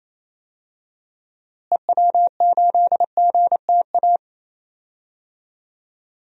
Morse code, heard as a keyed tone, about 28 wpm, 700 Hz. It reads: EW8GTA
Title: DXCC-100 Call Signs - 6 Characters - Repeated 3x - 28wpm